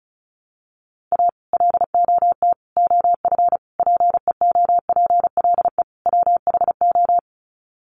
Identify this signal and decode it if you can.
Morse code, heard as a keyed tone, about 34 wpm, 700 Hz. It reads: A LOT OF PEOPLE WHO
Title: Sets of 5 Words 35wpm